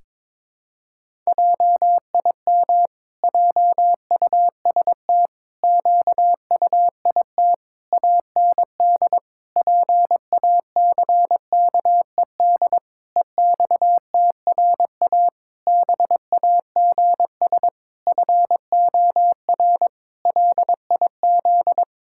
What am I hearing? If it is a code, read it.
Morse code, heard as a keyed tone, about 22 wpm, 700 Hz. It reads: JIM JUST QUIT AND PACKED EXTRA BAGS FOR LIZ
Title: Pangram Challenge v2 - 22wpm